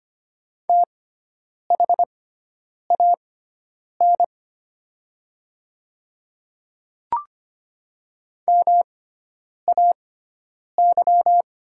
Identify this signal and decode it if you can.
Morse code, heard as a keyed tone, about 25 wpm, spaced as for 9 wpm, 700 Hz. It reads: THAN MAY
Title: 6x Character Spacing - Top 200 Words - 25wpm